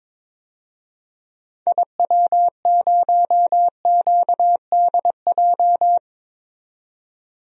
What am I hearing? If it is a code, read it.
Morse code, heard as a keyed tone, about 22 wpm, 700 Hz. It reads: IW0QDJ
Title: DXCC-100 Call Signs - 6 Characters - 22wpm